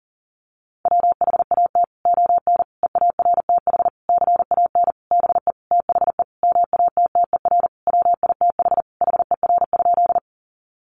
Morse code, heard as a keyed tone, about 40 wpm, 700 Hz. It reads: WHAT ON EARTH CAN BE THE MATTER WITH HER?
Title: Sentences from Top 700 Words 40wpm